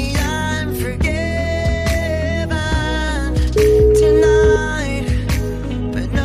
Ringback tone (Europe, 425 Hz), with noise.